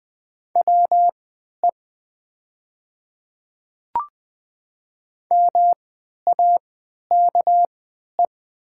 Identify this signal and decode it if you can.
Morse code, heard as a keyed tone, about 20 wpm, spaced as for 11 wpm, 700 Hz. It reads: WE MAKE
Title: 3x Character Spacing - Top 100 Words - 20wpm